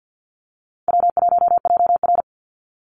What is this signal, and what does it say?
Morse code, heard as a keyed tone, about 54 wpm, 700 Hz. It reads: W1JR
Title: Easy Call Signs - 50wpm